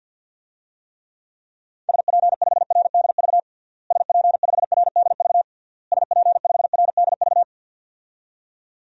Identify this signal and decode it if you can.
Morse code, heard as a keyed tone, about 50 wpm, 700 Hz. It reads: SP5RDV SP5RDV SP5RDV
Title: DXCC-100 Call Signs - 6 Characters - Repeated 3x - 50wpm